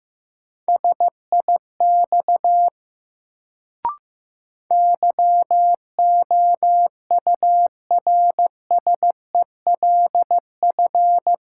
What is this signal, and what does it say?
Morse code, heard as a keyed tone, about 15 wpm, 700 Hz. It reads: SIX YOURSELF
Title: Sentences from Top 600 Words - New Word Drill - 15wpm